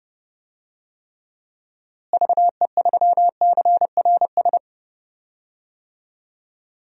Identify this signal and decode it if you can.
Morse code, heard as a keyed tone, about 30 wpm, 700 Hz. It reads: VE3CRS
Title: DXCC-100 Call Signs - 6 Characters - Repeated 2x - 30wpm